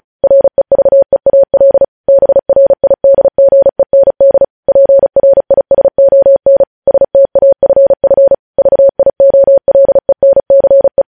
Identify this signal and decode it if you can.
Morse code, heard as a keyed tone, about 36 wpm, 550 Hz. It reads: REVEAL BRIDGEND PRISON STAFF VIOLENCE
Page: News Headlines in Morse Code at 35 WPM